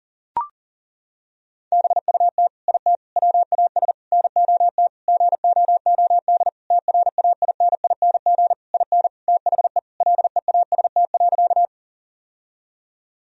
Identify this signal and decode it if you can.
Morse code, heard as a keyed tone, about 40 wpm, 700 Hz. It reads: BUT IT WAS NOT GOOD TRAINING IN THE LEAST.
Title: Sentences from Top 1000 Words - Review Part 4 - 40wpm